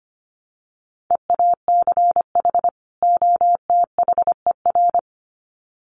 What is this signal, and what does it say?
Morse code, heard as a keyed tone, about 25 wpm, 700 Hz. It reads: EACH OTHER
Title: Sets of 2 Words 25wpm